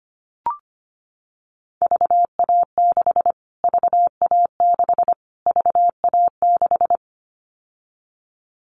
Morse code, heard as a keyed tone, about 25 wpm, 700 Hz. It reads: VA6 VA6 VA6
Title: Callsign Prefixes for Top 25 DXCC - Repeated 3x - 25wpm